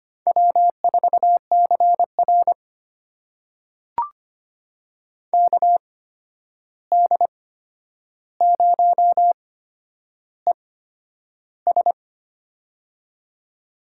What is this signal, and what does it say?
Morse code, heard as a keyed tone, about 25 wpm, spaced as for 10 wpm, 700 Hz. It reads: W4CR KD0ES
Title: USA Call Signs - 8x Character Spacing - 25wpm